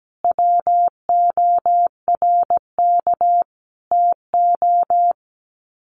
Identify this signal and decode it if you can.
Morse code, heard as a keyed tone, about 17 wpm, 700 Hz. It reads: WORK TO
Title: Sets of 5 Words Encore2 17wpm